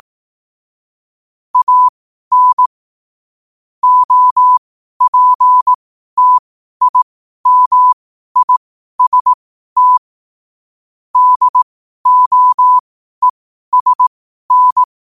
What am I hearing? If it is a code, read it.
Morse code, heard as a keyed tone, about 18 wpm, spaced as for 12 wpm, 1000 Hz. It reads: AN OPTIMIST DOESN
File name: qotdfiles / 12wpm